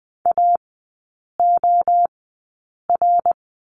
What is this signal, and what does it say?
Morse code, heard as a keyed tone, about 20 wpm, spaced as for 8 wpm, 700 Hz. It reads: AOR
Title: 060 Lesson 11e - Letter h - Three Characters 20wpm